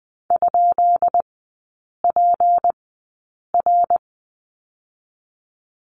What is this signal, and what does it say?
Morse code, heard as a keyed tone, about 20 wpm, spaced as for 8 wpm, 700 Hz. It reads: ?PR
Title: 205 Lesson 33e - Letter j - Three Characters 20wpm